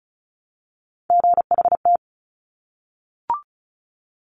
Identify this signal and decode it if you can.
Morse code, heard as a keyed tone, about 36 wpm, 700 Hz. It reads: GHT
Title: Top Three Letter Combinations in Words 35wpm